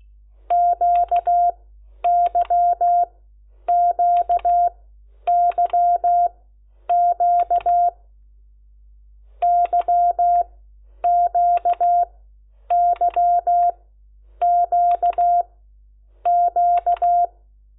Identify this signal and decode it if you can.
Morse code, heard as a keyed tone, about 16 wpm, spaced as for 9 wpm, 700 Hz. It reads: QYQYQ YQYQQ
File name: Lecciones / CLASE Nº10